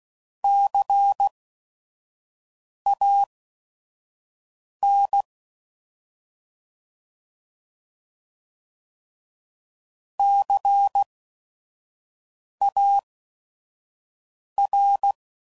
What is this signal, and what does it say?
Morse code, heard as a keyed tone, about 16 wpm, spaced as for 5 wpm, 800 Hz. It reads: CAN CAR
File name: Prefixes / By First Letter / Cs